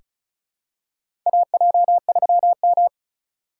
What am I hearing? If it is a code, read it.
Morse code, heard as a keyed tone, about 36 wpm, 700 Hz. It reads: AJ3M
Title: USA Call Signs - 6x Character Spacing - 35wpm